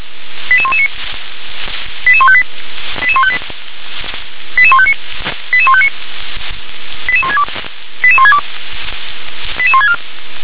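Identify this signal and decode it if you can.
Five-tone selective call (ZVEI), heard as quick runs of five tones, tones of about 70 ms. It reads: 88119 88158 88158 88159 88159 88162 88162 88164
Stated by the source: Five-tone ZVEI selective call received by SDR.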